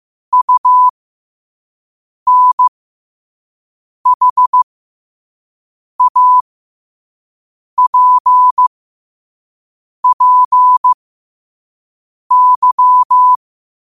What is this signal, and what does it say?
Morse code, heard as a keyed tone, about 15 wpm, spaced as for 5 wpm, 1000 Hz. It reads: UNHAPPY